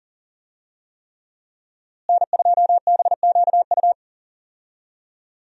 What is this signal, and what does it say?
Morse code, heard as a keyed tone, about 40 wpm, 700 Hz. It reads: N2BQU
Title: USA Call Signs - 5 Characters - Repeated 3x - 40wpm